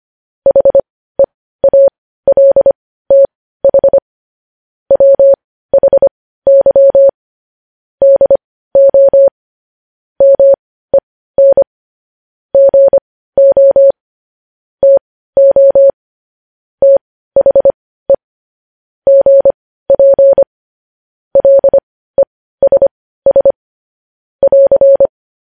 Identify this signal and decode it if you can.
Morse code, heard as a keyed tone, about 25 wpm, spaced as for 15 wpm, 550 Hz. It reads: HEALTH WHY DO MEN GO TO THE GP LESS <AR>
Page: News Headlines in Morse Code at 15 WPM